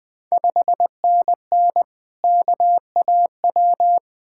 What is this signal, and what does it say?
Morse code, heard as a keyed tone, about 20 wpm, 700 Hz. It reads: 5NN KAW
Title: Ontario QSO Party Exchange 20wpm